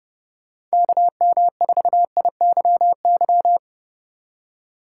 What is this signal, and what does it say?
Morse code, heard as a keyed tone, about 38 wpm, 700 Hz. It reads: KM4IYY